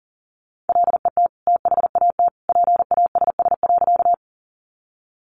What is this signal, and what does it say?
Morse code, heard as a keyed tone, about 40 wpm, 700 Hz. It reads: LET THAT PASS.
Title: Sentences from Top 400 Words 40wpm